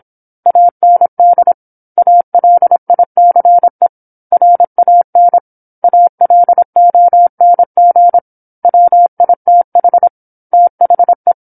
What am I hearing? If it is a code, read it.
Morse code, heard as a keyed tone, about 26 wpm, 700 Hz. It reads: AND ALICE RAN ALONG WITH THE
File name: 08-Chapter-08 26 WPM